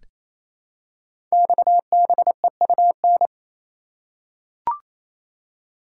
Morse code, heard as a keyed tone, about 27 wpm, 690 Hz. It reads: XBEUN
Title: Code Groups - 5 Letters - 28wpm